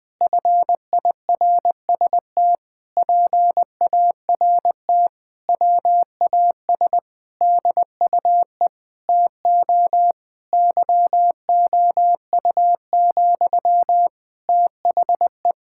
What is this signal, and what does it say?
Morse code, heard as a keyed tone, about 20 wpm, 700 Hz. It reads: FIRST PART WAS DUE TO YOU, THE
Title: Sentences from Top 1000 Words - Review Part 6 - 20wpm